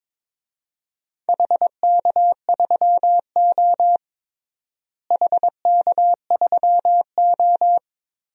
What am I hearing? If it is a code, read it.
Morse code, heard as a keyed tone, about 22 wpm, 700 Hz. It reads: HK3O HK3O
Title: DXCC-100 Call Signs - 4 Characters - Repeated 3x - 22wpm